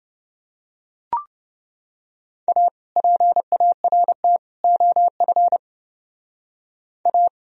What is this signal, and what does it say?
Morse code, heard as a keyed tone, about 28 wpm, 700 Hz. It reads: A PART OF A